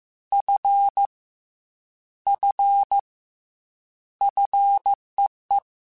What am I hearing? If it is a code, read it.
Morse code, heard as a keyed tone, about 15 wpm, 800 Hz. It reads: F F FEE
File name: Combinations / Combinations-mp3s / Combinations@13